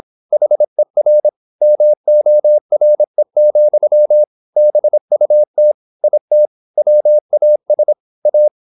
Morse code, heard as a keyed tone, about 26 wpm, 600 Hz. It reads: HER MORE, BUT IT WAS A
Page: Never 26wpm | CW med Gnister